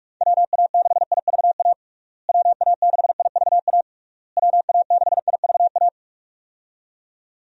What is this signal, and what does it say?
Morse code, heard as a keyed tone, about 47 wpm, 700 Hz. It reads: WA6IVA WA6IVA WA6IVA
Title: USA Call Signs - 6 Characters - Repeated 3x - 45wpm